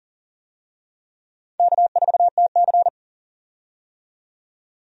Morse code, heard as a keyed tone, about 37 wpm, 700 Hz. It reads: K4TC
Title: USA Call Signs - 4 Characters - 40wpm